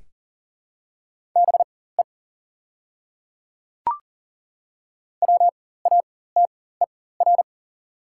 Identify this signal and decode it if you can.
Morse code, heard as a keyed tone, about 40 wpm, spaced as for 19 wpm, 690 Hz. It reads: BE WATER
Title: 4x Character Spacing - Top 200 Words - 40wpm